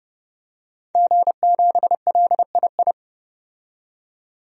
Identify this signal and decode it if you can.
Morse code, heard as a keyed tone, about 29 wpm, 690 Hz. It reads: G7LII
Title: DXCC-100 Call Signs - 5 Characters - 30wpm